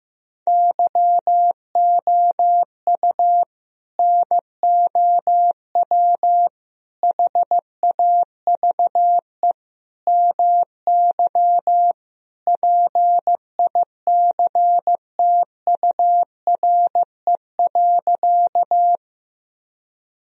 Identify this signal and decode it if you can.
Morse code, heard as a keyed tone, about 15 wpm, 700 Hz. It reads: YOU NOW HAVE MY PICTURE.
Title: Sentences from Top 1000 Words - Review Part 1 - 15wpm